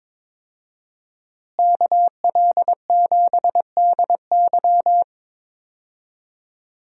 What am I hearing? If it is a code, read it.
Morse code, heard as a keyed tone, about 22 wpm, 700 Hz. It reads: KL7DY
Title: DXCC-100 Call Signs - 5 Characters - Repeated 2x - 22wpm